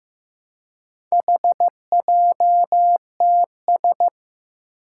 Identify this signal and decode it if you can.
Morse code, heard as a keyed tone, about 15 wpm, 700 Hz. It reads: HJTS
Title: Four Challenging Letter-Number Combinations 15wpm